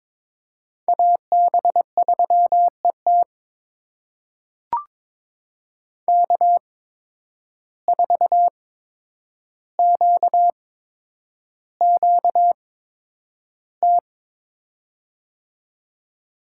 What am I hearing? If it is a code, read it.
Morse code, heard as a keyed tone, about 22 wpm, spaced as for 9 wpm, 700 Hz. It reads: AB3ET K4QQT
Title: USA Call Signs - 5 Characters - 8x Character Spacing - 22wpm